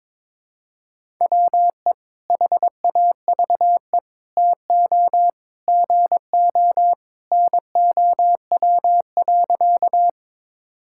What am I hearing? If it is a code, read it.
Morse code, heard as a keyed tone, about 22 wpm, 700 Hz. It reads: WE HAVE TO GO NOW.
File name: practice / 22wpm / Sentences with 3-Word Phrases - 22wpm